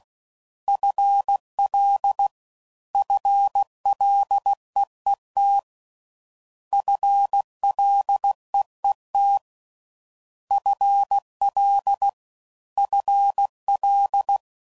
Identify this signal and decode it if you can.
Morse code, heard as a keyed tone, about 16 wpm, 800 Hz. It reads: FL FLEET FLEET FL FL